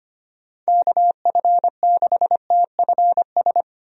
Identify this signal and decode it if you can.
Morse code, heard as a keyed tone, about 25 wpm, 700 Hz. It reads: KF6TFS
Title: USA Call Signs - 6 Characters - Repeated 2x - 25wpm